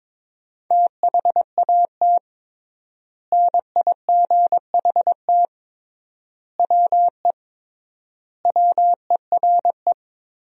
Morse code, heard as a keyed tone, about 21 wpm, 700 Hz. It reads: THAT NIGHT WE WERE
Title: 3x Word-Spacing - Sentences from Top 300 Words - 22wpm